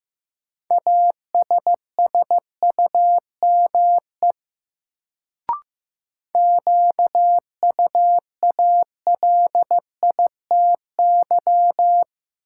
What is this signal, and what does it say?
Morse code, heard as a keyed tone, about 15 wpm, 700 Hz. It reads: ASSUME QUALITY
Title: Sentences from Top 1500 Words - New Word Drill Alt - 15wpm